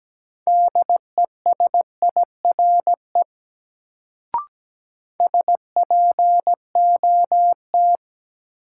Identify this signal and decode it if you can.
Morse code, heard as a keyed tone, about 17 wpm, 700 Hz. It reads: DESIRE SPOT
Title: Sentences from Top 1100 Words - New Word Drill Alt - 17wpm